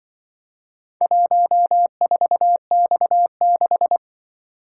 Morse code, heard as a keyed tone, about 23 wpm, 700 Hz. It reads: 14X6
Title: Speed-Racing - Four Letter-Number Combinations - 20wpm